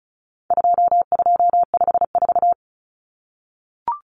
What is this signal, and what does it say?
Morse code, heard as a keyed tone, about 34 wpm, 700 Hz. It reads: 2254
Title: Common Cut Numbers - 4-Digit 35wpm